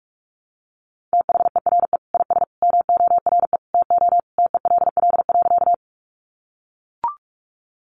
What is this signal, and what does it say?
Morse code, heard as a keyed tone, about 44 wpm, 700 Hz. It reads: THERE IS MORE TO TELL.
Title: Sentences from Top 200 Words 45wpm